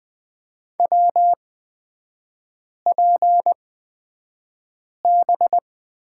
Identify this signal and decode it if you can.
Morse code, heard as a keyed tone, about 20 wpm, spaced as for 5 wpm, 700 Hz. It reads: WPB
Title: 250 Lesson 40e - BK - Three Characters 20wpm